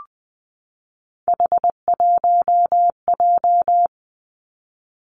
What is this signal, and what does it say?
Morse code, heard as a keyed tone, about 20 wpm, 700 Hz. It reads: H1J
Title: Three Challenging Letter-Number Combinations 20wpm